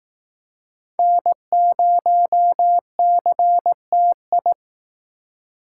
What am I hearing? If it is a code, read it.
Morse code, heard as a keyed tone, about 19 wpm, 700 Hz. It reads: N0CTI